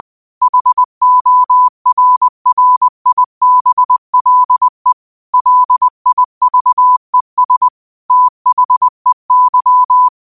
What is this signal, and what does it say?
Morse code, heard as a keyed tone, about 20 wpm, 1000 Hz. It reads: HORRIBLE LIVES THEY
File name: qotdfiles / 20wpm